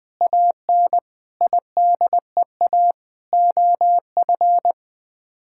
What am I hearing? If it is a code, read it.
Morse code, heard as a keyed tone, about 20 wpm, 700 Hz. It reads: AN IDEA OF
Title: Sets of 6 Words - 20wpm